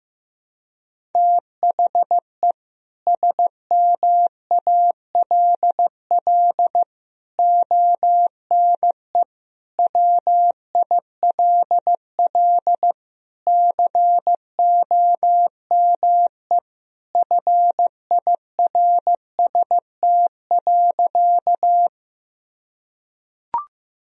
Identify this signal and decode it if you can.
Morse code, heard as a keyed tone, about 15 wpm, 700 Hz. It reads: THE SMALL ONE WILL COME FIRST.
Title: Sentences from Top 1000 Words - Review Part 1 - 15wpm